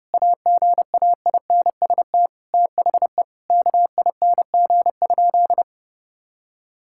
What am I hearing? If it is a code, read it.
Morse code, heard as a keyed tone, about 30 wpm, 700 Hz. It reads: AGAINST THE KING?
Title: Sentences from Top 1000 Words - Review Part 6 - 30wpm